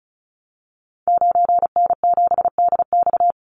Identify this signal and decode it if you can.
Morse code, heard as a keyed tone, about 34 wpm, 700 Hz. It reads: 9N7DX
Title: Call Signs Repeated 3x - 35wpm